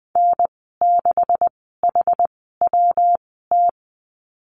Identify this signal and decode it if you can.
Morse code, heard as a keyed tone, about 20 wpm, spaced as for 15 wpm, 700 Hz. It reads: N6HWT
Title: 147 Lesson 24g - Letter y - Call Signs 20wpm